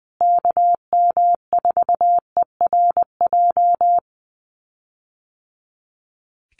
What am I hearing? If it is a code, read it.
Morse code, heard as a keyed tone, about 20 wpm, 700 Hz. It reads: KM4ERJ
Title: USA Call Signs Repeated 2x 20wpm